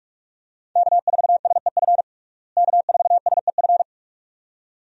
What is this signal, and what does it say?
Morse code, heard as a keyed tone, about 46 wpm, 700 Hz. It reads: K4SEF K4SEF
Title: USA Call Signs - 5 Characters - Repeated 2x - 45wpm